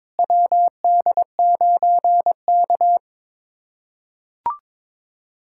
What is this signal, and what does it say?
Morse code, heard as a keyed tone, about 22 wpm, 700 Hz. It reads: WD9K
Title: USA Callsign DE Callsign - 4 Characters - 22wpm